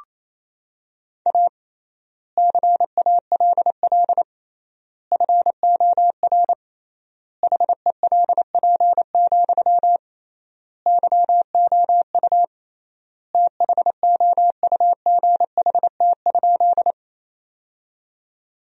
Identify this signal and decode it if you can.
Morse code, heard as a keyed tone, about 27 wpm, 700 Hz. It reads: A CALL FOR HELP, YOU THOUGHT?
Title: 3x Word-Spacing - Sentences from Top 300 Words - 28wpm